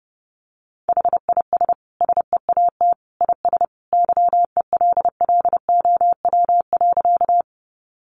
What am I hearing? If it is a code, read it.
Morse code, heard as a keyed tone, about 30 wpm, 700 Hz. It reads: HIS SEAT IS YELLOW.
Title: Additional Easy Phrases 30wpm